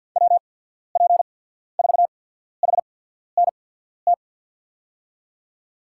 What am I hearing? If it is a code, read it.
Morse code, heard as a keyed tone, about 53 wpm, spaced as for 14 wpm, 700 Hz. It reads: WP4HNT
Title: USA Call Signs - 6 Characters - 8x Character Spacing - 50wpm